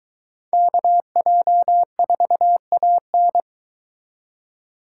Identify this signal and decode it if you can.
Morse code, heard as a keyed tone, about 22 wpm, 700 Hz. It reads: KJ4AN